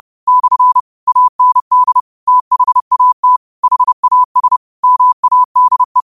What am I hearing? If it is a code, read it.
Morse code, heard as a keyed tone, about 30 wpm, 1000 Hz. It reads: / AND THAT HAS MADE
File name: qotdfiles / 30wpm